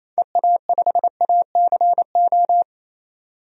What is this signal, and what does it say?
Morse code, heard as a keyed tone, about 28 wpm, 700 Hz. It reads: EA5ACO
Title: DXCC-100 Call Signs - 6 Characters - 28wpm